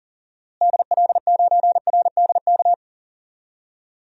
Morse code, heard as a keyed tone, about 40 wpm, 700 Hz. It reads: DL9RDK
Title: DXCC-100 Call Signs - 6 Characters - Repeated 3x - 40wpm